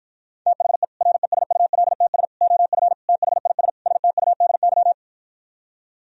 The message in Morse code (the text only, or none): THE RESULTS OF THIS STUDY